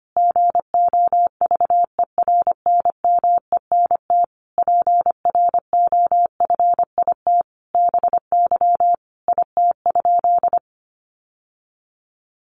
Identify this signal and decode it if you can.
Morse code, heard as a keyed tone, about 25 wpm, 700 Hz. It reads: GOVERNMENT PROFIT BY IT?
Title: Sentences from Top 800 Words 25wpm